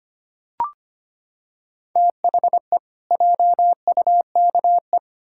Morse code, heard as a keyed tone, about 25 wpm, 700 Hz. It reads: THE JUKE
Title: Pangram Challenge v2 - 25wpm